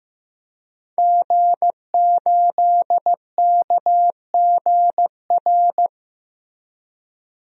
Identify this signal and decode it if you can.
Morse code, heard as a keyed tone, about 15 wpm, 700 Hz. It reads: G8KGR
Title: DXCC-100 Call Signs - 5 Characters - Repeated 2x - 15wpm